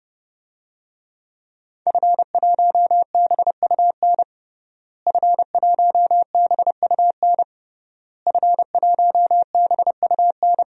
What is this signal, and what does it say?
Morse code, heard as a keyed tone, about 32 wpm, 700 Hz. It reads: F1BUN F1BUN F1BUN
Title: DXCC-100 Call Signs - 5 Characters - Repeated 3x - 30wpm